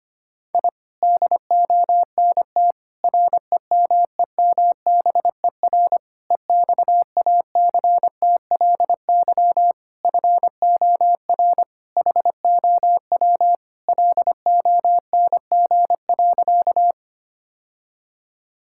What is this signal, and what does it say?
Morse code, heard as a keyed tone, about 25 wpm, 700 Hz. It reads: I DONT REMEMBER EXACTLY FOR HOW LONG.
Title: Sentences from Top 1000 Words - Review Part 6 - 25wpm